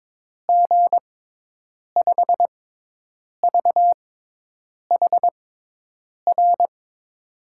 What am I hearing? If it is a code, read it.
Morse code, heard as a keyed tone, about 22 wpm, spaced as for 8 wpm, 700 Hz. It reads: G5VHR
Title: USA Call Signs - 6 Characters - 6x Character Spacing - 22wpm